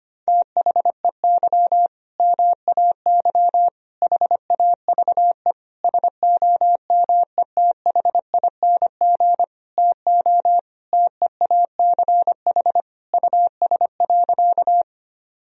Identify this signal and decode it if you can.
Morse code, heard as a keyed tone, about 25 wpm, 700 Hz. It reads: THEY MAY HAVE SOMETHING TO TEACH US.
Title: Sentences from Top 1000 Words - Review Part 2 - 25wpm